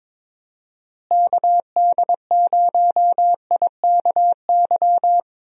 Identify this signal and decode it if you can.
Morse code, heard as a keyed tone, about 22 wpm, 700 Hz. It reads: KD0IKY
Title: USA Call Signs - 6 Characters - Repeated 3x - 22wpm